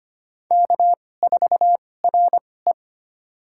Morse code, heard as a keyed tone, about 25 wpm, spaced as for 18 wpm, 700 Hz. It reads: K4RE
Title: USA Call Signs - 2x Character Spacing - 25wpm